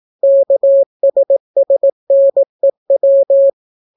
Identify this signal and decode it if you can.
Morse code, heard as a keyed tone, about 18 wpm, 550 Hz. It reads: KSSNEW